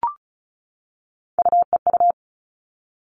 Morse code, heard as a keyed tone, about 34 wpm, 700 Hz. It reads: UEU
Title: All Cut Numbers - 3-Digit 35wpm